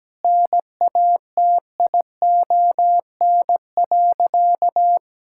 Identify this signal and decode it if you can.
Morse code, heard as a keyed tone, about 17 wpm, 700 Hz. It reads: NATION.
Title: Sentences from Top 1000 Words - Review Part 6 - 17wpm